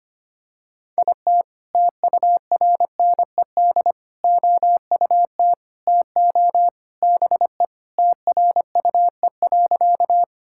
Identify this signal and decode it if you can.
Morse code, heard as a keyed tone, about 25 wpm, 700 Hz. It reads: IT TURNED OUT TO BE TRUE.
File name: practice / 25wpm / Sentences with 3-Word Phrases - 25wpm